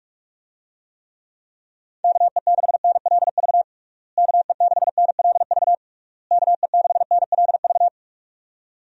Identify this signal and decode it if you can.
Morse code, heard as a keyed tone, about 47 wpm, 700 Hz. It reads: KE6NLV KE6NLV KE6NLV
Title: USA Call Signs - 6 Characters - Repeated 3x - 45wpm